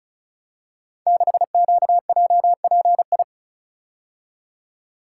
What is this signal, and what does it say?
Morse code, heard as a keyed tone, about 36 wpm, 700 Hz. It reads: 6QJPI
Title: Code Groups - 5 Letter:Numbers - 35wpm